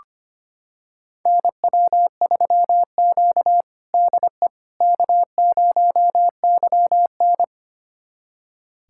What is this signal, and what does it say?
Morse code, heard as a keyed tone, about 25 wpm, 700 Hz. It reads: NW3Q DE K0YN
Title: USA Callsign DE Callsign - 4 Characters - 25wpm